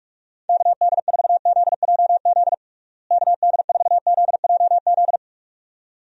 Morse code, heard as a keyed tone, about 46 wpm, 700 Hz. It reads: KD4ZJZ KD4ZJZ
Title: USA Call Signs - 6 Characters - Repeated 2x - 45wpm